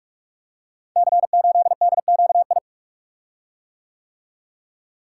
Morse code, heard as a keyed tone, about 44 wpm, 700 Hz. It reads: C8DQI
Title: Code Groups - 5 Letter:Numbers - 45wpm